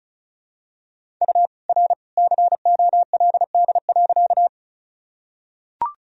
Morse code, heard as a keyed tone, about 36 wpm, 700 Hz. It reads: U R COLD.
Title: Sentences from Top 400 Words and Abbreviations 35wpm